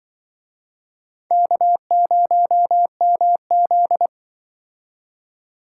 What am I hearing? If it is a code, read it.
Morse code, heard as a keyed tone, about 25 wpm, 700 Hz. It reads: K0MZ